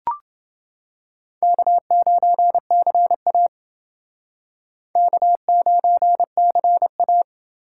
Morse code, heard as a keyed tone, about 28 wpm, 700 Hz. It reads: K9CA K9CA